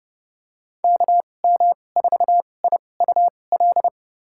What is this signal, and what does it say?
Morse code, heard as a keyed tone, about 30 wpm, spaced as for 22 wpm, 700 Hz. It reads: KM4IUL
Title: USA Call Signs - 2x Character Spacing - 30wpm